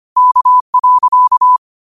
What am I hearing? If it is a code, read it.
Morse code, heard as a keyed tone, about 25 wpm, 1000 Hz. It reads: K.
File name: qotdfiles / 25wpm